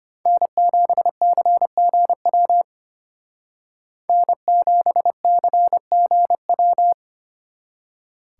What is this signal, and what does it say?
Morse code, heard as a keyed tone, about 32 wpm, 700 Hz. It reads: N7CGW N7CGW